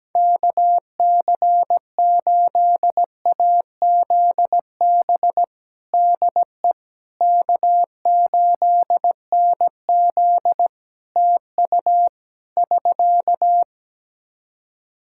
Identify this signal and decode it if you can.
Morse code, heard as a keyed tone, about 17 wpm, 700 Hz. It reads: KC8AZB DE K8NZ TU <SK>
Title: Practice QSO - Exchange 4 - Ending - 17wpm